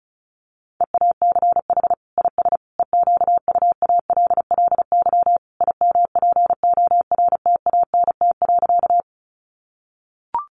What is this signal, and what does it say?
Morse code, heard as a keyed tone, about 34 wpm, 700 Hz. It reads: EACH IS EQUALLY IMPORTANT.
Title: Sentences from Top 1400 Words 35wpm